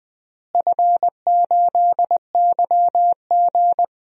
Morse code, heard as a keyed tone, about 20 wpm, 700 Hz. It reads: F8YG
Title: DXCC-100 Call Signs - 4 Characters - Repeated 3x - 20wpm